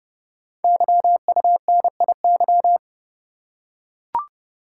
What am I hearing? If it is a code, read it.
Morse code, heard as a keyed tone, about 30 wpm, 700 Hz. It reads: YUNIY
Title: Code Groups - 5 Letters - 30wpm